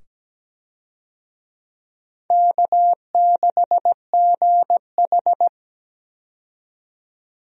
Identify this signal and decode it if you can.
Morse code, heard as a keyed tone, about 17 wpm, 700 Hz. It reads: K6GH
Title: USA Call Signs - 4 Characters - 17wpm